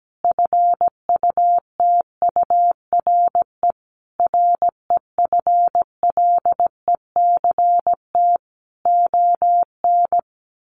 Morse code, heard as a keyed tone, about 17 wpm, 700 Hz. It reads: FUTURE REFLECT ON
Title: Sentences from Top 1500 Words 17wpm